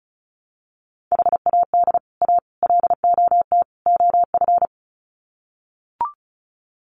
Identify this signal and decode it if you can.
Morse code, heard as a keyed tone, about 34 wpm, 700 Hz. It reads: HAD A LOT OF
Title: Sets of 4 Words 35wpm